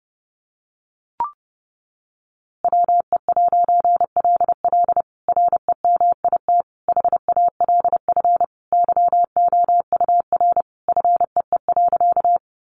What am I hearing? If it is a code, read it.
Morse code, heard as a keyed tone, about 30 wpm, 700 Hz. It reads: WE'LL REMIT HALF YOUR FEE.
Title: A Sound Of Thunder 30wpm